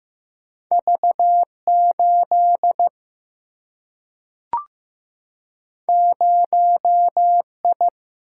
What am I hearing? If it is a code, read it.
Morse code, heard as a keyed tone, about 15 wpm, 700 Hz. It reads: V8 0I
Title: Single-Letter Single-Number 15wpm